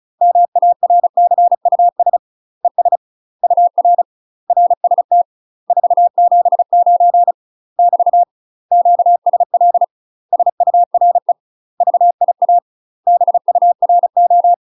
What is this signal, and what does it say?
Morse code, heard as a keyed tone, about 36 wpm, 700 Hz. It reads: MARCUS ES UR RST 479 <BT> QSL SURE VIA BURO